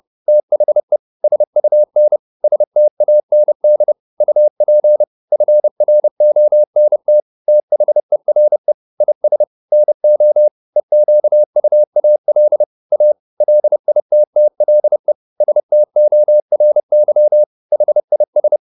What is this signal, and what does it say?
Morse code, heard as a keyed tone, about 30 wpm, 600 Hz. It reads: THE SUN STAND UP FRONT THERE IS NO EQUAL A LITTLE STORY HIS